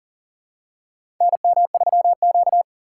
Morse code, heard as a keyed tone, about 40 wpm, 700 Hz. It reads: NM3Q
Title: USA Call Signs - 3x Character Spacing - 40wpm